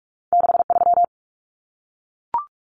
Morse code, heard as a keyed tone, about 46 wpm, 700 Hz. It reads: 63
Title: Numbers - 2-Digit 45wpm